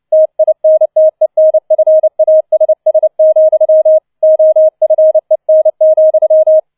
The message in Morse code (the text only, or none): TINTENFASS, OFEN,